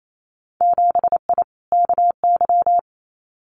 Morse code, heard as a keyed tone, about 28 wpm, 700 Hz. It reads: 7I KY
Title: Winter Field Day Exchanges Encore 28wpm